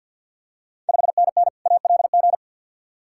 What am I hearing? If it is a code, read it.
Morse code, heard as a keyed tone, about 50 wpm, 690 Hz. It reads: 5NN ALG
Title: Ontario QSO Party Exchange 50wpm